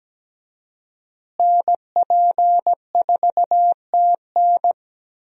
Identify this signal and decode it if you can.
Morse code, heard as a keyed tone, about 17 wpm, 700 Hz. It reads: NP4TN
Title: USA Call Signs - 5 Characters - Repeated 2x - 17wpm